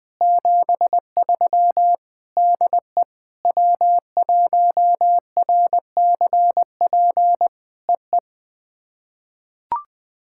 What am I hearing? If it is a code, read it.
Morse code, heard as a keyed tone, about 20 wpm, 700 Hz. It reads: 73 DE W1RCP EE
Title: POTA QSO Practice - 20wpm